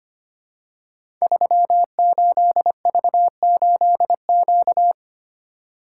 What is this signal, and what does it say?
Morse code, heard as a keyed tone, about 25 wpm, 700 Hz. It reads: 38V8Q
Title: Code Groups - 5 Letter:Numbers - 25wpm